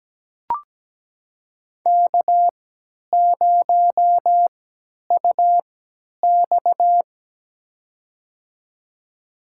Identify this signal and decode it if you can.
Morse code, heard as a keyed tone, about 17 wpm, spaced as for 10 wpm, 700 Hz. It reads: K0UX
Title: USA Call Signs - 4 Characters - 3x Character Spacing - 17wpm